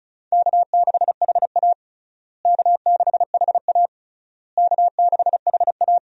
Morse code, heard as a keyed tone, about 36 wpm, 700 Hz. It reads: K6HA K6HA K6HA
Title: USA Call Signs - 4 Characters - Repeated 3x - 35wpm